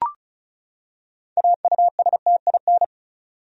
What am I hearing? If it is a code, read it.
Morse code, heard as a keyed tone, about 34 wpm, 700 Hz. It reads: AUSTIN
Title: USA Common Names - 35wpm